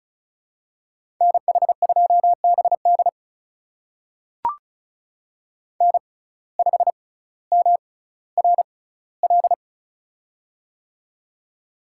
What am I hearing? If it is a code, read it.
Morse code, heard as a keyed tone, about 36 wpm, spaced as for 18 wpm, 700 Hz. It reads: NH2BD N5MRL
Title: USA Call Signs - 5 Characters - 6x Character Spacing - 35wpm